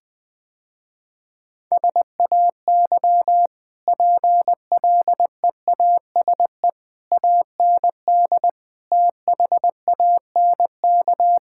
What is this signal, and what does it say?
Morse code, heard as a keyed tone, about 21 wpm, 700 Hz. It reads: SAY PLEASE AND THANK
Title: Easy Sentences with Hints - 20wpm